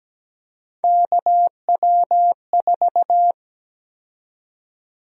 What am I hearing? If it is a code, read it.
Morse code, heard as a keyed tone, about 17 wpm, 700 Hz. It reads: KW4
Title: USA Call Sign Prefixes - Repeated 2x - 17wpm